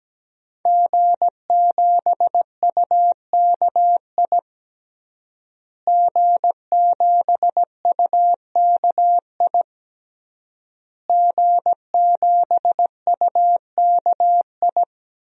Morse code, heard as a keyed tone, about 17 wpm, 700 Hz. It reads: G7UKI G7UKI G7UKI
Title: DXCC-100 Call Signs - 5 Characters - Repeated 3x - 17wpm